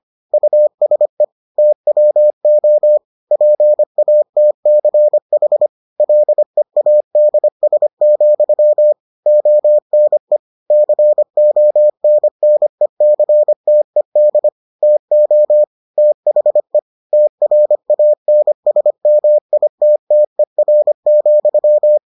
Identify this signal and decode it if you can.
Morse code, heard as a keyed tone, about 25 wpm, 600 Hz. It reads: USE TWO PATCH LEADS, ONE CONNECTED TO THE TRANSMITTER,